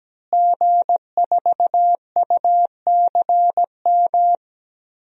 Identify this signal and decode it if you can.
Morse code, heard as a keyed tone, about 17 wpm, 700 Hz. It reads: G4UCM
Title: DXCC-100 Call Signs - 5 Characters - Repeated 2x - 17wpm